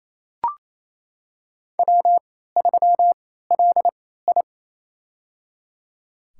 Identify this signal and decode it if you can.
Morse code, heard as a keyed tone, about 28 wpm, spaced as for 16 wpm, 700 Hz. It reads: W3LI
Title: USA Call Signs - 3x Character Spacing - 28wpm